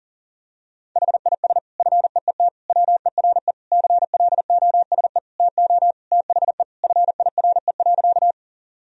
Morse code, heard as a keyed tone, about 40 wpm, 700 Hz. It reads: HIS FEET WERE CLOSE TO THE FIRE.
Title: Sentences from Top 1000 Words - Review Part 1 - 40wpm